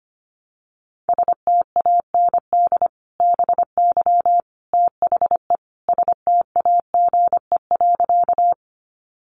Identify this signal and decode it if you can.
Morse code, heard as a keyed tone, about 25 wpm, 700 Hz. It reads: STAND BY THE STAGE.
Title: Sentences from Top 1200 Words 25wpm